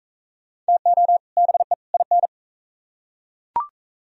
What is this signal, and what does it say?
Morse code, heard as a keyed tone, about 41 wpm, 700 Hz. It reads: TO BE IN